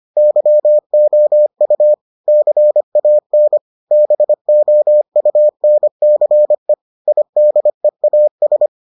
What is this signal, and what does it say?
Morse code, heard as a keyed tone, about 25 wpm, 600 Hz. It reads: YOU CAN BOUNCE IDEAS